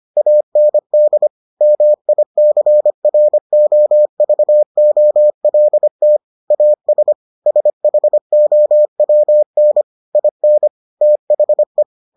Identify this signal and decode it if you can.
Morse code, heard as a keyed tone, about 25 wpm, 600 Hz. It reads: AND MICROVOLT AS SHOWN IN THE